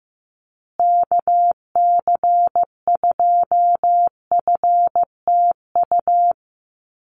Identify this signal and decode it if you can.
Morse code, heard as a keyed tone, about 15 wpm, 700 Hz. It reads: KC2FTU
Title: USA Call Signs Repeated 2x 15wpm